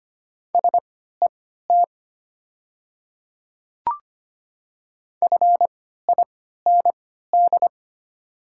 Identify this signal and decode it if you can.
Morse code, heard as a keyed tone, about 25 wpm, spaced as for 14 wpm, 700 Hz. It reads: SET FIND
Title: 3x Character Spacing - Top 200 Words - 25wpm